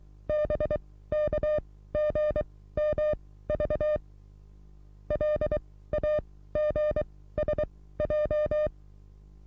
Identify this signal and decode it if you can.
Morse code, heard as a keyed tone, about 24 wpm, spaced as for 16 wpm, 600 Hz. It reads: BKGMV LAGSJ
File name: repos / Morsekurs / DJ4UF Morsen27